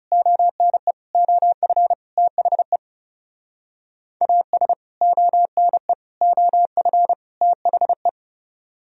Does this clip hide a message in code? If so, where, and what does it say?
Yes: Morse code from 0.1 to 8.1 s, heard as a keyed tone, about 39 wpm, 700 Hz. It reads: ONE OF THE AS ONE OF THE